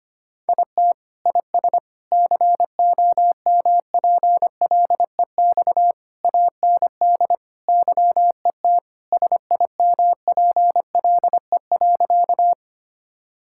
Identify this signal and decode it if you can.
Morse code, heard as a keyed tone, about 25 wpm, 700 Hz. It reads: IT IS COMPLEX AND YET SIMPLE.
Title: Sentences from Top 1000 Words - Review Part 6 - 25wpm